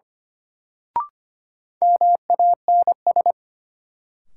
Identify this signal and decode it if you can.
Morse code, heard as a keyed tone, about 25 wpm, 700 Hz. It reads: MANS
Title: Sentences from Top 1000 Words - Review Part 6 - New Word Drill - 25wpm